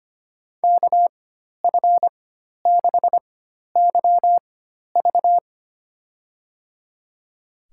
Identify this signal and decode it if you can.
Morse code, heard as a keyed tone, about 25 wpm, spaced as for 12 wpm, 700 Hz. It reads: KF6YV
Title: USA Call Signs - 5 Characters - 4x Character Spacing - 25wpm